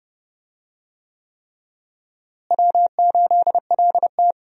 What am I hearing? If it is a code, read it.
Morse code, heard as a keyed tone, about 30 wpm, 700 Hz. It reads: W8LT
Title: USA Call Signs - 4 Characters - Repeated 3x - 30wpm